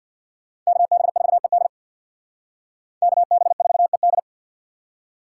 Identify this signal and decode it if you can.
Morse code, heard as a keyed tone, about 62 wpm, 700 Hz. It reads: KB4ED KB4ED
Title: Speed Racing - USA 2x2 Call Signs 50wpm